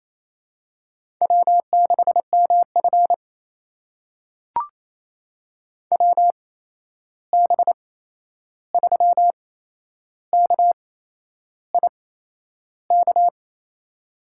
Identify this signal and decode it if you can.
Morse code, heard as a keyed tone, about 28 wpm, spaced as for 11 wpm, 700 Hz. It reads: W6MF WB3KIK
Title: USA Call Signs - 8x Character Spacing - 28wpm